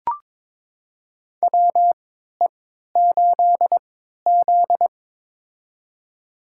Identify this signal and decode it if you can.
Morse code, heard as a keyed tone, about 21 wpm, spaced as for 12 wpm, 700 Hz. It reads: WE8Z
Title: USA Call Signs - 4 Characters - 3x Character Spacing - 22wpm